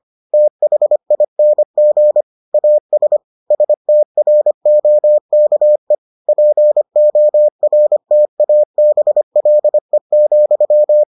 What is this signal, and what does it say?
Morse code, heard as a keyed tone, about 25 wpm, 600 Hz. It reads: THING AS STROKE PORTABLE,